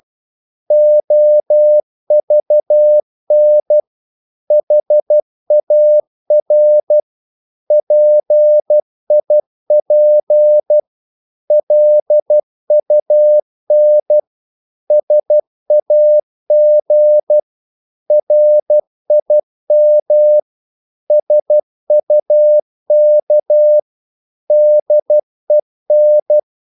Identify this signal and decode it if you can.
Morse code, heard as a keyed tone, about 12 wpm, 600 Hz. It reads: OVN HAR PIP LUN SAG RIM SUK DEN
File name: Korte ord DK 12wpm